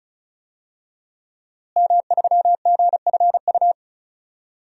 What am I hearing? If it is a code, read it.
Morse code, heard as a keyed tone, about 36 wpm, 700 Hz. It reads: M3GFU
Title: DXCC-100 Call Signs - 5 Characters - Repeated 2x - 35wpm